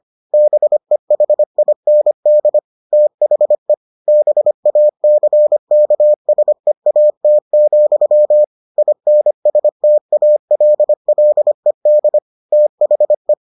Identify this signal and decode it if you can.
Morse code, heard as a keyed tone, about 25 wpm, 600 Hz. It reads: BEHIND THE BACKSEAT, INSTALLED THE